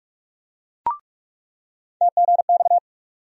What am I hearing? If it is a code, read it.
Morse code, heard as a keyed tone, about 42 wpm, 700 Hz. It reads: TGX
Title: USA Call Sign Postfixes - 3 Letters 45wpm